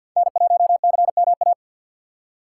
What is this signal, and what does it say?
Morse code, heard as a keyed tone, about 53 wpm, 700 Hz. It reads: N1CMA
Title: USA Call Signs - 4x Character Spacing - 50wpm